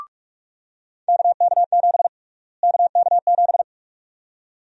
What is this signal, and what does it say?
Morse code, heard as a keyed tone, about 45 wpm, 700 Hz. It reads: KK7 KK7
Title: USA Call Sign Prefixes - Repeated 2x - 45wpm